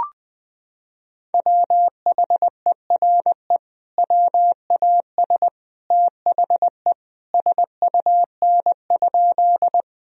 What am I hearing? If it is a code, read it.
Morse code, heard as a keyed tone, about 20 wpm, 700 Hz. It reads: WHERE WAS THE SUN?
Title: Sentences from Top 1000 Words - Review Part 1 - 20wpm